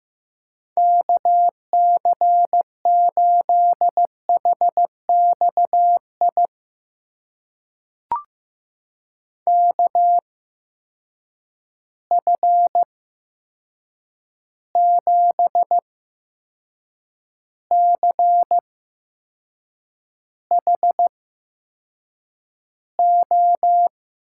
Morse code, heard as a keyed tone, about 15 wpm, spaced as for 6 wpm, 700 Hz. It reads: KC8HXI KF7CHO
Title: USA Call Signs - 6 Characters - 8x Character Spacing - 15wpm